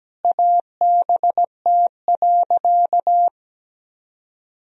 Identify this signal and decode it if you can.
Morse code, heard as a keyed tone, about 17 wpm, 700 Hz. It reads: ABT.
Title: Sentences from Top 400 Words and Abbreviations 17wpm